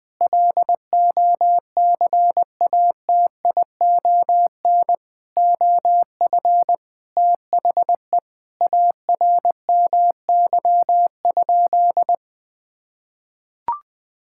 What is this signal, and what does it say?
Morse code, heard as a keyed tone, about 20 wpm, 700 Hz. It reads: LOCATION OF THE ARMY?
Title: Sentences from Top 1000 Words - Review Part 6 - 20wpm